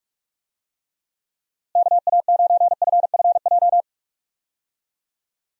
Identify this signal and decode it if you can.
Morse code, heard as a keyed tone, about 46 wpm, 700 Hz. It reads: KA9FFJ
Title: USA Call Signs - 6 Characters - Repeated 3x - 45wpm